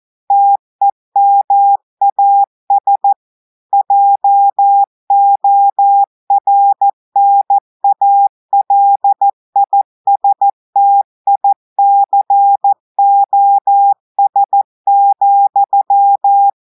Morse code, heard as a keyed tone, about 14 wpm, 800 Hz. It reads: TEMAS JORNALISTICOS,
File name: audios / portugais / portugais texte en clair 2 vitesse 14 mots minutes